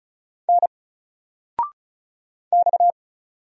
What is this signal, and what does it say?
Morse code, heard as a keyed tone, about 36 wpm, 700 Hz. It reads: N X
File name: Single-Letter-Number-35wpm